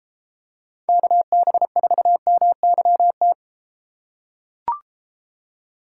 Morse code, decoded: KB4MYT

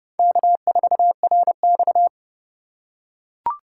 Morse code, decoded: K4RX

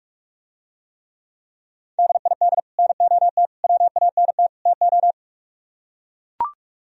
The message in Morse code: DID NOT WANT TO

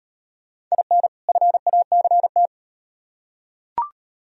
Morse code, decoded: IN FACT

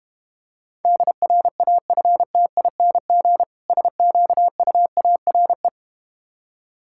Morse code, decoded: DRAFTING SQUARE

